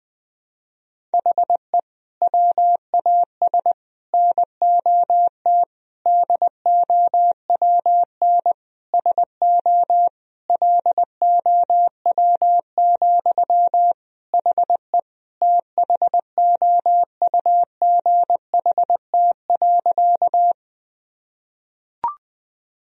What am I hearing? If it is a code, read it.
Morse code, heard as a keyed tone, about 20 wpm, 700 Hz. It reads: HE WAS NOT DOWN SO LOW, HE THOUGHT.
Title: Sentences from Top 1000 Words - Review Part 1 - 20wpm